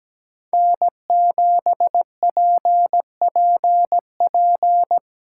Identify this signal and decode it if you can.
Morse code, heard as a keyed tone, about 19 wpm, 700 Hz. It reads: N7PPP
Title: Speed Racing - USA 1x3 Call Signs 17wpm